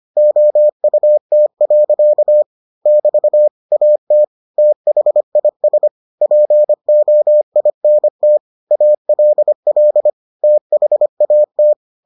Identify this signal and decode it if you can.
Morse code, heard as a keyed tone, about 25 wpm, 600 Hz. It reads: OUT. <BT> AT THIS POINT ALL THAT